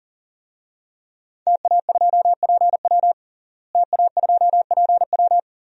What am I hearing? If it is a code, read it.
Morse code, heard as a keyed tone, about 40 wpm, 700 Hz. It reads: TA2PW TA2PW
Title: DXCC-100 Call Signs - 5 Characters - Repeated 2x - 40wpm